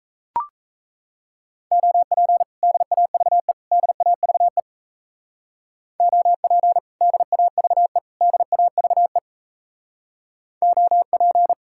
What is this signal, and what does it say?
Morse code, heard as a keyed tone, about 34 wpm, 700 Hz. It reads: OP DAVE DAVE OP DAVE DAVE OP
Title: Speed-Racing Common QSO Phrases - 28wpm